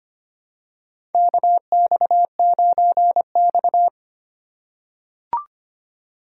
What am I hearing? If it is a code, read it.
Morse code, heard as a keyed tone, about 25 wpm, 700 Hz. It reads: KX9X
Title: USA Call Signs - 4 Characters - 4x Character Spacing - 25wpm